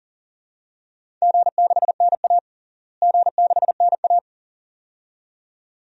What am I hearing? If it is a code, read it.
Morse code, heard as a keyed tone, about 40 wpm, 700 Hz. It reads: G6NA G6NA
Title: DXCC-100 Call Signs - 4 Characters - Repeated 2x - 40wpm